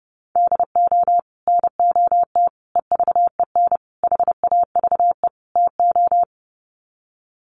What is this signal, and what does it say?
Morse code, heard as a keyed tone, about 28 wpm, 700 Hz. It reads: DO NOT EVEN HAVE TO
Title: Sets of 5 Words Encore2 30wpm